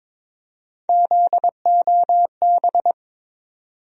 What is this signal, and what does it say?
Morse code, heard as a keyed tone, about 21 wpm, 700 Hz. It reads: ZOB
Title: USA Call Sign Postfixes - 3 Letters 22wpm